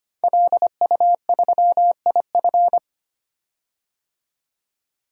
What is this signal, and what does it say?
Morse code, heard as a keyed tone, about 25 wpm, 700 Hz. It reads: LU3IF
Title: DXCC-100 Call Signs - 5 Characters - 25wpm